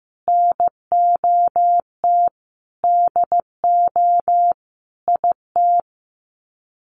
Morse code, heard as a keyed tone, about 15 wpm, 700 Hz. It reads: NOT DO IT
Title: Sets of 5 Words Encore2 15wpm